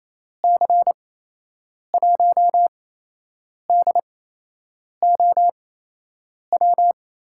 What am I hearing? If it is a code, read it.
Morse code, heard as a keyed tone, about 28 wpm, spaced as for 8 wpm, 700 Hz. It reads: C1DOW
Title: USA Call Signs - 8x Character Spacing - 28wpm